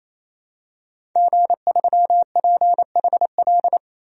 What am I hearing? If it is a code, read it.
Morse code, heard as a keyed tone, about 28 wpm, 700 Hz. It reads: G3PHL
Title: DXCC-100 Call Signs - 5 Characters - 28wpm